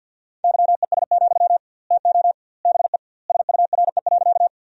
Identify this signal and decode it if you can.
Morse code, heard as a keyed tone, about 52 wpm, 700 Hz. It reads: YES, TO BE SURE.
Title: 2x Word-Spacing - Sentences from Top 500 Words - 50wpm